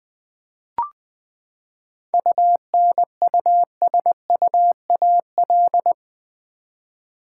Morse code, heard as a keyed tone, about 20 wpm, 700 Hz. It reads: UNUSUAL
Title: Seven Letter Words - 20wpm